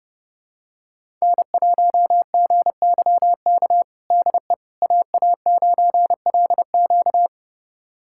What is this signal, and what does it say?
Morse code, heard as a keyed tone, about 30 wpm, 700 Hz. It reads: N1GYK DE AA9LQ
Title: USA Callsign DE Callsign - 5 Characters - 30wpm